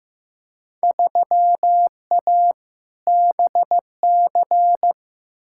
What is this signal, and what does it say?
Morse code, heard as a keyed tone, about 15 wpm, 700 Hz. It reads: 3A BC
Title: Field Day Exchanges - 15wpm